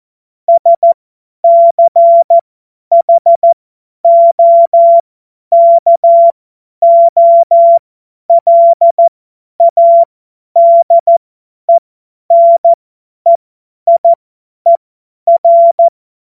Morse code, heard as a keyed tone, about 14 wpm, spaced as for 10 wpm, 680 Hz. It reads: SCHOKOLADENEIER
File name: Morsekurs / Witze / Witz 07. März 2023 50 F70